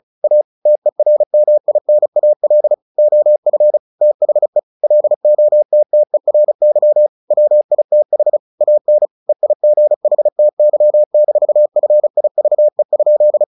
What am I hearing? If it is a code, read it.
Morse code, heard as a keyed tone, about 34 wpm, 600 Hz. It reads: A TERMINAL OF THE LOTTERY WITH AN EIGHTY-FIVE?
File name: the old man and the sea / the old man and the sea 1 35wpm